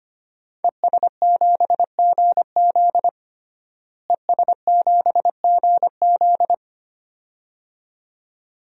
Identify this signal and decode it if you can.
Morse code, heard as a keyed tone, about 25 wpm, 700 Hz. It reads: ES7GZ ES7GZ
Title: DXCC-100 Call Signs - 5 Characters - Repeated 2x - 25wpm